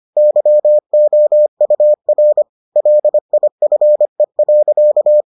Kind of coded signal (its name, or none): Morse code